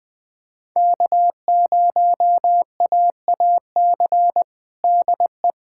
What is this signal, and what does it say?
Morse code, heard as a keyed tone, about 20 wpm, 700 Hz. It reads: K0AAC DE
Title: USA Callsign DE Callsign - 5 Characters - 20wpm